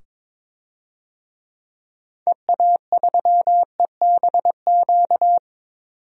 Morse code, heard as a keyed tone, about 21 wpm, 700 Hz. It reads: EA3EBQ
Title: DXCC-100 Call Signs - 6 Characters - 22wpm